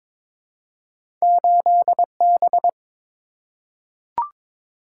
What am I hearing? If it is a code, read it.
Morse code, heard as a keyed tone, about 22 wpm, 700 Hz. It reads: 8B
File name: practice / 22wpm / Two Letter-Number Combinations - 22wpm